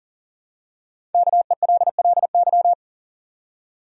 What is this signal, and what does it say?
Morse code, heard as a keyed tone, about 40 wpm, 700 Hz. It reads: KELLY
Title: USA Common Names - Max 5 Letters - 40wpm